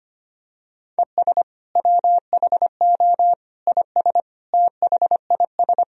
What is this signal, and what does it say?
Morse code, heard as a keyed tone, about 25 wpm, 700 Hz. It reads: ES WHO IS THIS
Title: Sentences from Top 300 Words and Abbreviations 25wpm